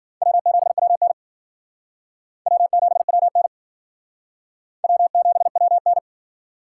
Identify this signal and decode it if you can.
Morse code, heard as a keyed tone, about 52 wpm, 700 Hz. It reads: W7WN W7WN W7WN